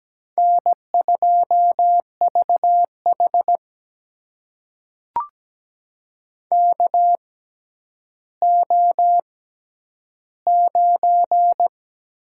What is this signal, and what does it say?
Morse code, heard as a keyed tone, about 17 wpm, spaced as for 10 wpm, 700 Hz. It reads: N2VH KO9
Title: USA Call Signs - 4 Characters - 6x Character Spacing - 17wpm